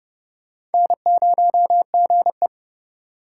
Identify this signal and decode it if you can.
Morse code, heard as a keyed tone, about 31 wpm, 700 Hz. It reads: N0GE
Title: USA Call Signs - 4 Characters - 2x Character Spacing - 30wpm